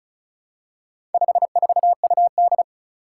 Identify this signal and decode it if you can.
Morse code, heard as a keyed tone, about 36 wpm, 700 Hz. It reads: 54UD
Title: Four Challenging Letter-Number Combinations 35wpm